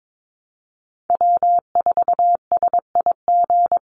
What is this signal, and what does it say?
Morse code, heard as a keyed tone, about 22 wpm, 700 Hz. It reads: W4SIG
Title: Call Signs Repeated 3x - 22wpm